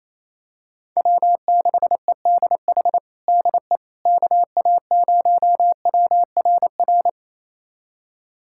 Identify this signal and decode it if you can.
Morse code, heard as a keyed tone, about 28 wpm, 700 Hz. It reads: W6EDH DE KA0WRR
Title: USA Callsign DE Callsign 28wpm